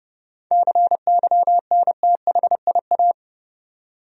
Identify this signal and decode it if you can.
Morse code, heard as a keyed tone, about 30 wpm, 700 Hz. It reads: CYNTHIA
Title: USA Common Names - 30wpm